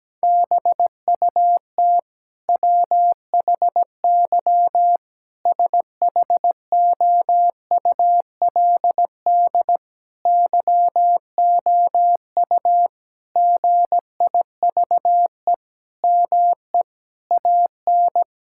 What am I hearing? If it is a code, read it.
Morse code, heard as a keyed tone, about 17 wpm, 700 Hz. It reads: BUT WHY SHOULD YOU GIVE ME AN
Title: Sentences from Top 1000 Words - Review Part 6 - 17wpm